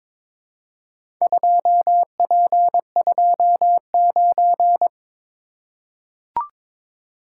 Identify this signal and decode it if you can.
Morse code, heard as a keyed tone, about 21 wpm, 700 Hz. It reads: 2P29
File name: practice / 22wpm / Four Letter-Number Combinations - 22wpm